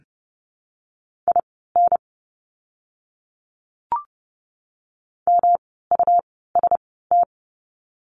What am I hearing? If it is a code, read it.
Morse code, heard as a keyed tone, about 30 wpm, spaced as for 17 wpm, 700 Hz. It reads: IN MUST
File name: course / 30:10wpm / 237 Lesson 38f - Letter q - Words 30wpm